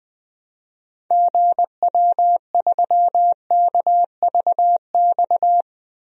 Morse code, heard as a keyed tone, about 21 wpm, 700 Hz. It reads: GW3KVX
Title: DXCC-100 Call Signs - 6 Characters - 20wpm